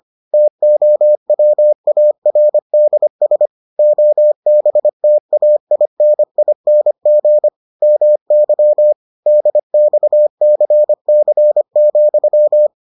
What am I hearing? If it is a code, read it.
Morse code, heard as a keyed tone, about 25 wpm, 600 Hz. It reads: TOWARDS OBTAINING MY DXCC,